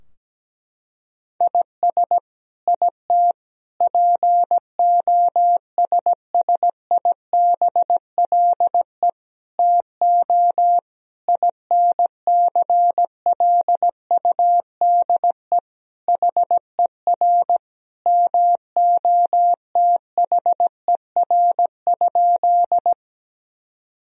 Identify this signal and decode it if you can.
Morse code, heard as a keyed tone, about 17 wpm, 700 Hz. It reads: IS IT POSSIBLE TO INCLUDE HER MOTHER?
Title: Sentences from Top 1000 Words - Review Part 3 - 17wpm